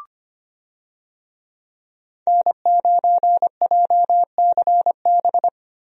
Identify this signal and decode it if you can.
Morse code, heard as a keyed tone, about 25 wpm, 700 Hz. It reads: N9JCB
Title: USA Call Signs - 5 Characters - Repeated 3x - 25wpm